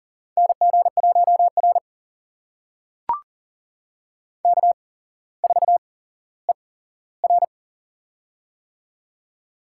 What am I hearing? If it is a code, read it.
Morse code, heard as a keyed tone, about 44 wpm, spaced as for 18 wpm, 700 Hz. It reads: NG1R K4ER
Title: USA Call Signs - 8x Character Spacing - 40wpm